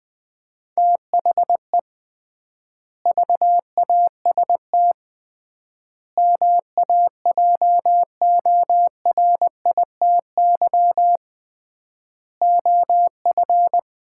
Morse code, heard as a keyed tone, about 20 wpm, 700 Hz. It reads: THE VAST MAJORITY OF
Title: 3x Word-Spacing - Sets of 4 Words - 20wpm